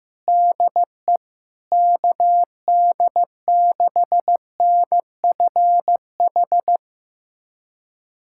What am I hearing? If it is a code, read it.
Morse code, heard as a keyed tone, about 16 wpm, 700 Hz. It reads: DE KD6NFH
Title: USA Callsign DE Callsign - 6 Characters - 15wpm